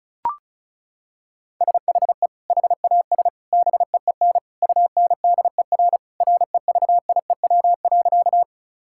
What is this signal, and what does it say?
Morse code, heard as a keyed tone, about 34 wpm, 700 Hz. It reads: SHE HAS BEEN UNDER REVIEW.
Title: Sentences from Top 1000 Words - Review Part 6 - 35wpm